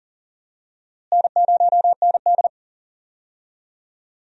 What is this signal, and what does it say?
Morse code, heard as a keyed tone, about 38 wpm, 700 Hz. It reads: N0ND
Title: USA Call Signs - 4 Characters - Repeated 3x - 40wpm